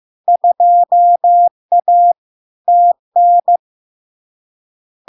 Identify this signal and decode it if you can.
Morse code, heard as a keyed tone, about 15 wpm, 700 Hz. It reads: 2A TN
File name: LARK-FD-2022-Log-in-CW-15wpm-700-Hz